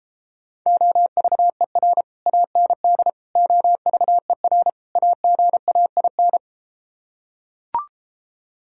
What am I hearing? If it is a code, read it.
Morse code, heard as a keyed tone, about 34 wpm, 700 Hz. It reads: OVER AND OVER AGAIN